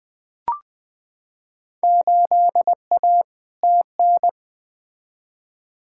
Morse code, heard as a keyed tone, about 20 wpm, 700 Hz. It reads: 8A TN
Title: Field Day Exchanges Encore - 20wpm